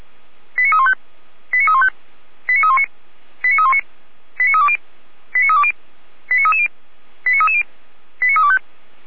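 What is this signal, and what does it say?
Five-tone selective call (ZVEI), heard as quick runs of five tones, tones of about 70 ms. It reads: 89315 89315 89319 89319 89320 89320 89330 89330 89325
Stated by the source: Five-tone ZVEI selective call 89337 received by SDR, from audio.f85360.feuerwehr-orlamuende.org fire brigade server.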